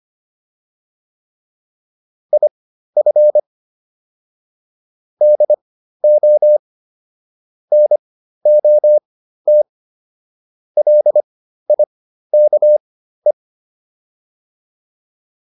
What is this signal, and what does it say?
Morse code, heard as a keyed tone, about 25 wpm, spaced as for 13 wpm, 600 Hz. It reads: IF DO NOT LIKE